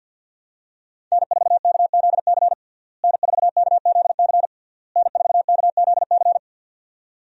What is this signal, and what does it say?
Morse code, heard as a keyed tone, about 50 wpm, 700 Hz. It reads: N4KZC N4KZC N4KZC
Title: USA Call Signs - 5 Characters - Repeated 3x - 50wpm